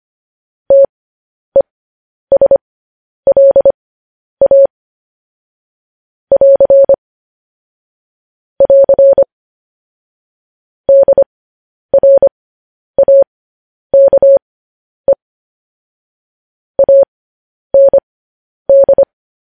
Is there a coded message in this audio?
Yes: Morse code from 0.7 to 19.0 s, heard as a keyed tone, about 25 wpm, spaced as for 10 wpm, 550 Hz. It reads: TESLA <AR> <AR> DRAKE AND